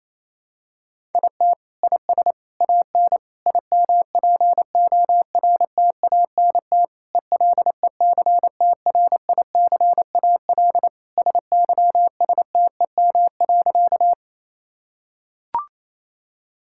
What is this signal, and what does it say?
Morse code, heard as a keyed tone, about 28 wpm, 700 Hz. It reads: IT IS AN IMPORTANT ELECTRICAL SYSTEM.
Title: Sentences from Top 1000 Words - Review Part 6 - 28wpm